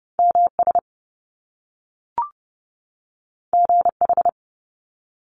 Morse code, heard as a keyed tone, about 30 wpm, 700 Hz. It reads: MS GH
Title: Top Two Letter Combinations in Words 30wpm